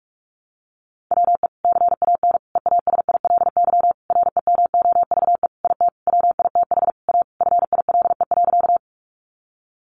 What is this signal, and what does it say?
Morse code, heard as a keyed tone, about 44 wpm, 700 Hz. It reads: WE CAN EASILY REMOVE IT WITH A FILE.
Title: Sentences from Top 1200 Words 45wpm